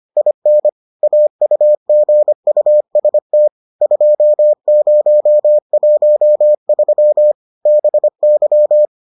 Morse code, heard as a keyed tone, about 25 wpm, 600 Hz. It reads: IN AUGUST 2013 BY